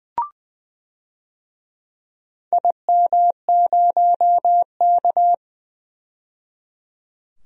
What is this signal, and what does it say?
Morse code, heard as a keyed tone, about 20 wpm, 700 Hz. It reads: IM0K
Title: DXCC-100 Call Signs - 4 Characters - 20wpm